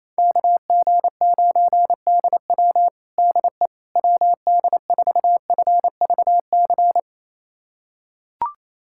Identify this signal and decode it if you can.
Morse code, heard as a keyed tone, about 28 wpm, 700 Hz. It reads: KG9DW DE WD4FVC
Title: USA Callsign DE Callsign 28wpm